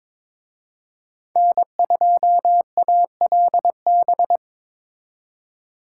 Morse code, heard as a keyed tone, about 25 wpm, 700 Hz. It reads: N2ALB
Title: Speed Racing - USA 1x3 Call Signs 22wpm